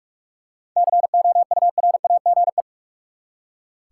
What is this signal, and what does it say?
Morse code, heard as a keyed tone, about 44 wpm, 700 Hz. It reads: COURAGE
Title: Seven Letter Words - 45wpm